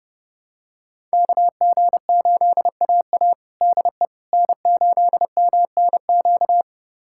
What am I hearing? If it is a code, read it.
Morse code, heard as a keyed tone, about 32 wpm, 700 Hz. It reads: KG8AA DE N8MNQ
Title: USA Callsign DE Callsign - 5 Characters - 30wpm